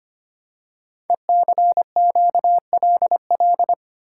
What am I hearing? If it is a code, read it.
Morse code, heard as a keyed tone, about 25 wpm, 700 Hz. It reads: ECQLL
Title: Code Groups - 5 Letters - 25wpm